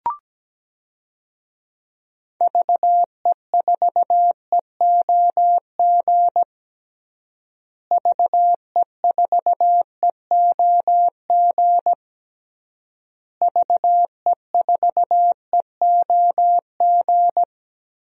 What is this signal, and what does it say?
Morse code, heard as a keyed tone, about 17 wpm, 700 Hz. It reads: VE4EOG VE4EOG VE4EOG
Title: DXCC-100 Call Signs - 6 Characters - Repeated 3x - 17wpm